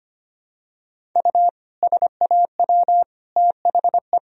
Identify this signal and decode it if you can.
Morse code, heard as a keyed tone, about 25 wpm, 700 Hz. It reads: U SAW THE
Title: Sentences from Top 400 Words and Abbreviations 25wpm